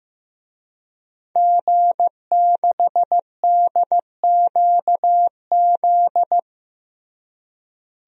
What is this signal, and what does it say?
Morse code, heard as a keyed tone, about 15 wpm, 700 Hz. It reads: G6DQZ
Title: DXCC-100 Call Signs - 5 Characters - Repeated 2x - 15wpm